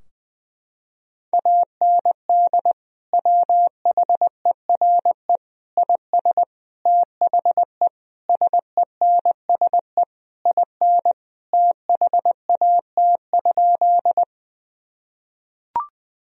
Morse code, heard as a keyed tone, about 20 wpm, 700 Hz. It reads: AND WHERE IS THE SENSE IN THAT?
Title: Sentences from Top 1000 Words - Review Part 4 - 20wpm